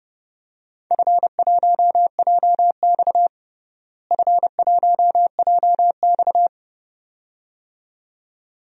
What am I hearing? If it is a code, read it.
Morse code, heard as a keyed tone, about 30 wpm, 700 Hz. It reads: F1JX F1JX
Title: DXCC-100 Call Signs - 4 Characters - Repeated 2x - 30wpm